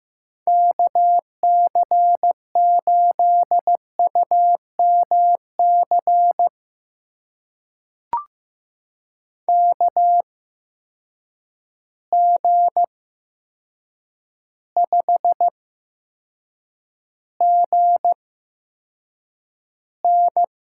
Morse code, heard as a keyed tone, about 15 wpm, spaced as for 7 wpm, 700 Hz. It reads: KC8UMC KG5GN
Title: USA Call Signs - 6 Characters - 8x Character Spacing - 15wpm